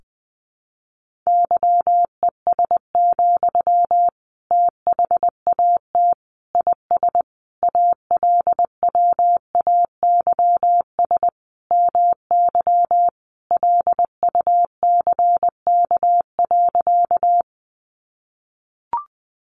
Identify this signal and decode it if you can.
Morse code, heard as a keyed tone, about 20 wpm, 700 Hz. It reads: YES, THAT IS ALWAYS MY LUCK.
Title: Sentences from Top 1500 Words 20wpm